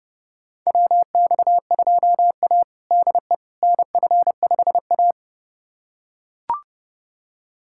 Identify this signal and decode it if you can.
Morse code, heard as a keyed tone, about 30 wpm, 700 Hz. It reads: WX2A DE NF5A
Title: USA Callsign DE Callsign - 4 Characters - 30wpm